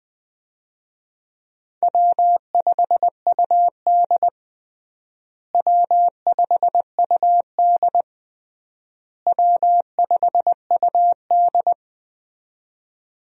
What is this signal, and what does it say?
Morse code, heard as a keyed tone, about 20 wpm, 700 Hz. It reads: W5UD W5UD W5UD
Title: USA Call Signs - 4 Characters - Repeated 3x - 20wpm